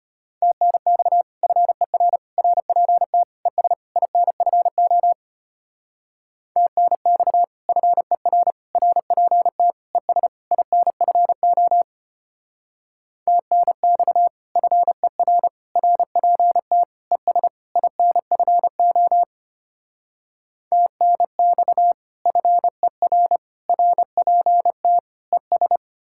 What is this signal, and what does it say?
Morse code, heard as a keyed tone, about 33 wpm, 700 Hz. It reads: TNX FER RPT ES INFO TNX FER RPT ES INFO TNX FER RPT ES INFO TNX FER RPT ES